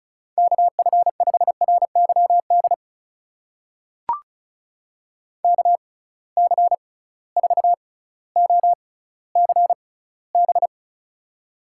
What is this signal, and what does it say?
Morse code, heard as a keyed tone, about 36 wpm, spaced as for 18 wpm, 700 Hz. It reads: KF5RYD KC4OCB
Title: USA Call Signs - 6 Characters - 6x Character Spacing - 35wpm